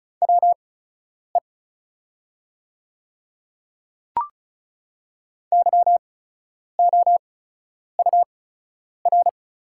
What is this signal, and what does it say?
Morse code, heard as a keyed tone, about 34 wpm, spaced as for 10 wpm, 690 Hz. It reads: WE YOUR
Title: 8x Character Spacing - Top 300 Words - 35wpm